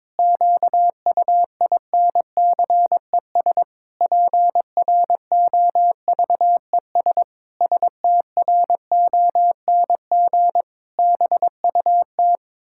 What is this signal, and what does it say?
Morse code, heard as a keyed tone, about 22 wpm, 700 Hz. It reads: QUINCES PROVES STRONG BUT
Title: Pangram Challenge v2 - 22wpm